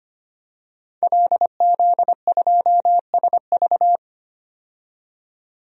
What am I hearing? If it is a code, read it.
Morse code, heard as a keyed tone, about 25 wpm, 700 Hz. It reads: LZ2SV
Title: DXCC-100 Call Signs - 5 Characters - 25wpm